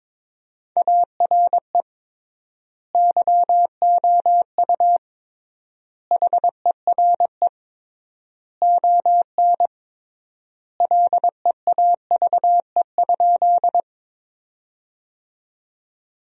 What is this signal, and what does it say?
Morse code, heard as a keyed tone, about 22 wpm, 700 Hz. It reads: ARE YOU HERE ON LEAVE?
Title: 3x Word-Spacing - Sentences from Top 400 Words - 22wpm